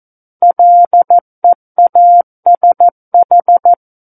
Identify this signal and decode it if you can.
Morse code, heard as a keyed tone, about 14 wpm, 700 Hz. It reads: LEASH
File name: Book One / 05-Chapter 5 - 14 WPM